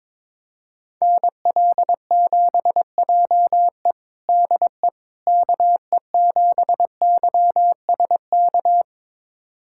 Morse code, heard as a keyed tone, about 22 wpm, 700 Hz. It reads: NL7JE DE KE7YSK
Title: USA Callsign DE Callsign 22wpm